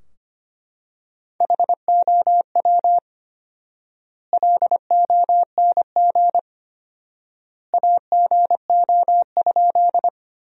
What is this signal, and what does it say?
Morse code, heard as a keyed tone, about 25 wpm, 700 Hz. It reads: HOW LONG AGO?
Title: 4x Word-Spacing - Sentences from Top 500 Words - 25wpm